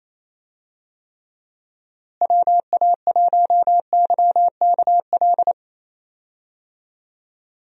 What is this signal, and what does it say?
Morse code, heard as a keyed tone, about 28 wpm, 700 Hz. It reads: WA1YKL
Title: USA Call Signs - 6 Characters - Repeated 2x - 28wpm